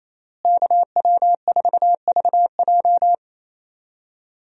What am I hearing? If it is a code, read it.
Morse code, heard as a keyed tone, about 27 wpm, 700 Hz. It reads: KW4VJ
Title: USA Call Signs - 5 Characters - Repeated 3x - 28wpm